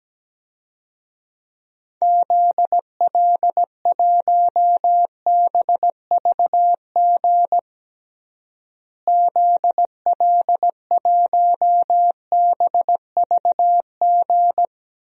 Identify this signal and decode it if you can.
Morse code, heard as a keyed tone, about 17 wpm, 700 Hz. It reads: ZL1BVG ZL1BVG
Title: DXCC-100 Call Signs - 6 Characters - Repeated 3x - 17wpm